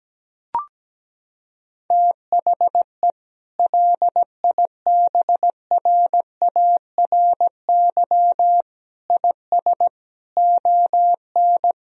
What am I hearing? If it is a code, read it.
Morse code, heard as a keyed tone, about 17 wpm, 700 Hz. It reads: THE LIBRARY IS ON
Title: Sentences from Top 1000 Words - Review Part 4 - 17wpm